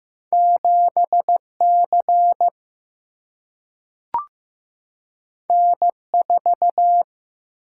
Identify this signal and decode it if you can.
Morse code, heard as a keyed tone, about 15 wpm, 700 Hz. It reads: 7 C N 4
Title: Single-Letter Single-Number 15wpm